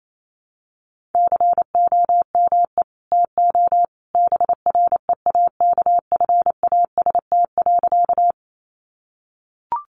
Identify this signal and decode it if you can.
Morse code, heard as a keyed tone, about 28 wpm, 700 Hz. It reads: COME TO BREAKFAST.
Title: Sentences from Top 1500 Words 28wpm